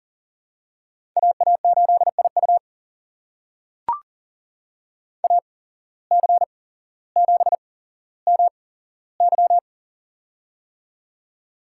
Morse code, heard as a keyed tone, about 40 wpm, spaced as for 17 wpm, 700 Hz. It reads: AA8IU AC7MY
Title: USA Call Signs - 5 Characters - 8x Character Spacing - 40wpm